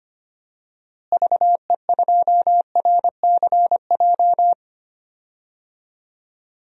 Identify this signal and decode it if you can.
Morse code, heard as a keyed tone, about 25 wpm, 700 Hz. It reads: VE2RCJ
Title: DXCC-100 Call Signs - 6 Characters - Repeated 2x - 25wpm